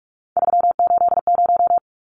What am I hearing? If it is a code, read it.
Morse code, heard as a keyed tone, about 44 wpm, 700 Hz. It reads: 380
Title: Numbers - 3-Digit 45wpm